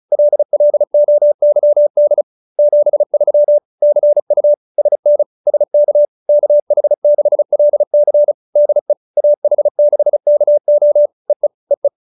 Morse code, heard as a keyed tone, about 34 wpm, 590 Hz. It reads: LLOYD 73 CU SN SK KH6LC DE AH6KO EE EE